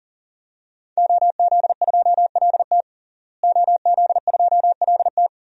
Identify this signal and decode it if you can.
Morse code, heard as a keyed tone, about 37 wpm, 700 Hz. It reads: OZ2LT OZ2LT
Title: DXCC-100 Call Signs - 5 Characters - Repeated 2x - 40wpm